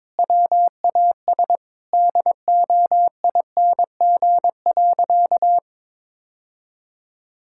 Morse code, heard as a keyed tone, about 22 wpm, 700 Hz. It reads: WAS DOING.
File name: practice / 22wpm / Sentences with 8-Word Phrases - 22wpm